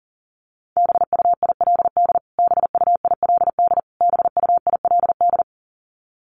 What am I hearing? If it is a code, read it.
Morse code, heard as a keyed tone, about 40 wpm, 700 Hz. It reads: BUILD BUILD BUILD
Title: Top 200 Words Repeated 40wpm